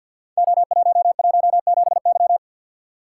Morse code, heard as a keyed tone, about 47 wpm, 700 Hz. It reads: O117Y
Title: Code Groups - 5 Letter:Numbers - 50wpm